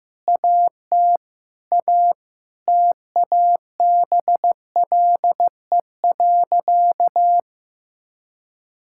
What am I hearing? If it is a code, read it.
Morse code, heard as a keyed tone, about 16 wpm, 700 Hz. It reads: AT A TABLE.
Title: Sentences from Top 1600 Words 15wpm